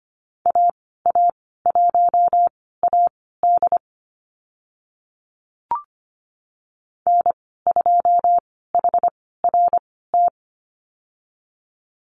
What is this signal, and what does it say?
Morse code, heard as a keyed tone, about 25 wpm, spaced as for 16 wpm, 700 Hz. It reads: AA1AD N2HRT
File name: Lesson 15g - Number 5 - Call Signs 25wpm